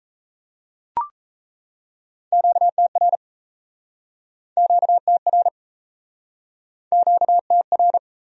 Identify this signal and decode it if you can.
Morse code, heard as a keyed tone, about 40 wpm, 700 Hz. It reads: QTR QTR QTR